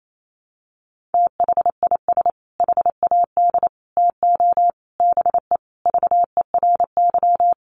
Morse code, heard as a keyed tone, about 28 wpm, 700 Hz. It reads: THIS HAD TO BE VERY
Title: Sentences from Top 1300 Words 28wpm